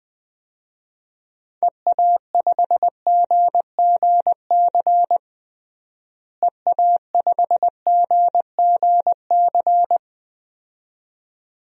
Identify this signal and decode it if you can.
Morse code, heard as a keyed tone, about 20 wpm, 700 Hz. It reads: EA5GGC EA5GGC
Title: DXCC-100 Call Signs - 6 Characters - Repeated 2x - 20wpm